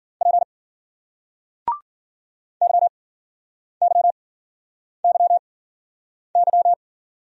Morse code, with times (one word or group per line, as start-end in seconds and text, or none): 0.2-0.4 P
2.6-6.7 YYYY